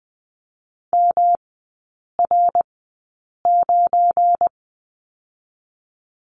Morse code, heard as a keyed tone, about 20 wpm, 700 Hz. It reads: M R 9
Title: 218 Lesson 35e - Number 0 - Three Characters 20wpm